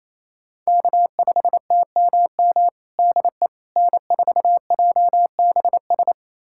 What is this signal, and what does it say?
Morse code, heard as a keyed tone, about 27 wpm, 700 Hz. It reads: K5TMM DE N4JBS
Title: USA Callsign DE Callsign - 5 Characters - 28wpm